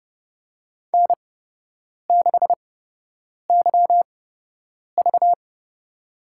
Morse code, heard as a keyed tone, about 30 wpm, spaced as for 8 wpm, 700 Hz. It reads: N6YV
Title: USA Call Signs - 4 Characters - 8x Character Spacing - 30wpm